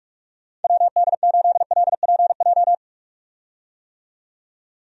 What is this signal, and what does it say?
Morse code, heard as a keyed tone, about 42 wpm, 700 Hz. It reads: WD8LPJ
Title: USA Call Signs - 6 Characters - 45wpm